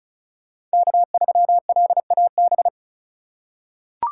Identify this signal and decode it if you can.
Morse code, heard as a keyed tone, about 36 wpm, 700 Hz. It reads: K3LAB
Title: USA Call Signs - 5 Characters - 4x Character Spacing - 35wpm